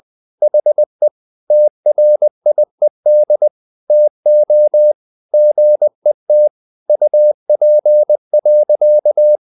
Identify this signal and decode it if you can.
Morse code, heard as a keyed tone, about 20 wpm, 600 Hz. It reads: HE TRIED TO GET UP.
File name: the old man and the sea part 2 20wpm